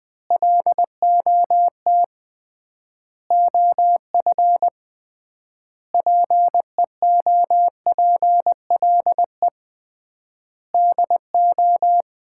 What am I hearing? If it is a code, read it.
Morse code, heard as a keyed tone, about 20 wpm, 700 Hz. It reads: LOT OF PEOPLE DO
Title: 3x Word-Spacing - Sets of 5 Words - 20wpm